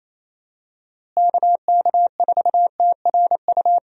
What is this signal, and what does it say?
Morse code, heard as a keyed tone, about 27 wpm, 700 Hz. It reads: KK4TRU
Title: USA Call Signs - 6 Characters - Repeated 3x - 28wpm